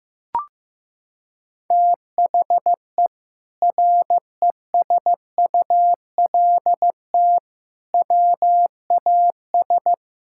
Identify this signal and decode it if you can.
Morse code, heard as a keyed tone, about 16 wpm, 700 Hz. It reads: THE RESULT WAS
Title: Sentences from Top 1000 Words - Review Part 6 - 15wpm